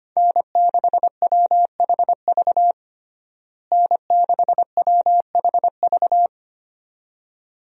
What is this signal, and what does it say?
Morse code, heard as a keyed tone, about 25 wpm, 700 Hz. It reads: N6WHV N6WHV
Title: USA Call Signs - 5 Characters - Repeated 2x - 25wpm